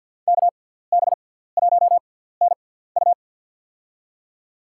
Morse code, heard as a keyed tone, about 47 wpm, spaced as for 17 wpm, 700 Hz. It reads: KB1NU
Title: USA Call Signs - 5 Characters - 6x Character Spacing - 50wpm